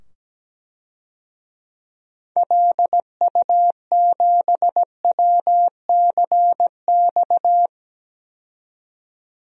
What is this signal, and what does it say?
Morse code, heard as a keyed tone, about 17 wpm, 700 Hz. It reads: LU7WCX
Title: DXCC-100 Call Signs - 6 Characters - 17wpm